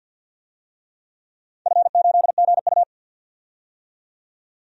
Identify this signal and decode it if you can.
Morse code, heard as a keyed tone, about 52 wpm, 700 Hz. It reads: F8GU
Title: DXCC-100 Call Signs - 4 Characters - 50wpm